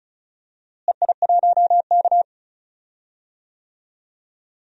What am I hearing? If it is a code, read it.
Morse code, heard as a keyed tone, about 36 wpm, 700 Hz. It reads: EI1K
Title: DXCC-100 Call Signs - 4 Characters - Repeated 3x - 35wpm